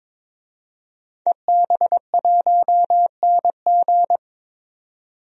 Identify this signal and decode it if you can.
Morse code, heard as a keyed tone, about 21 wpm, 700 Hz. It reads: EB1NG
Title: DXCC-100 Call Signs - 5 Characters - 22wpm